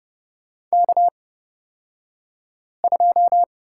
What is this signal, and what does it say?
Morse code, heard as a keyed tone, about 30 wpm, spaced as for 5 wpm, 700 Hz. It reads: K2
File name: Lesson 38d - Letter q - Two Characters 30wpm